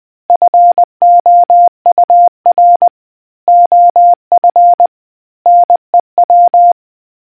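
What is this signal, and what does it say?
Morse code, heard as a keyed tone, about 20 wpm, 700 Hz. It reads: FOUR OF NEW
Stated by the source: Morse quiz question for naval history buffs from the RNZN Communicators Association.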